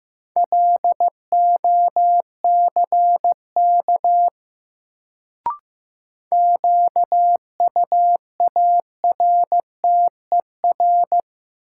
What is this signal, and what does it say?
Morse code, heard as a keyed tone, about 15 wpm, 700 Hz. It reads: LOCK QUARTER
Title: Sentences from Top 1300 Words - New Word Drill Alt - 15wpm